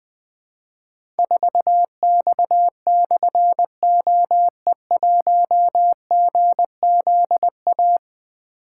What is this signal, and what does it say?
Morse code, heard as a keyed tone, about 21 wpm, 700 Hz. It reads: 4X/OE1GZA
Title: Difficult Call Signs 20wpm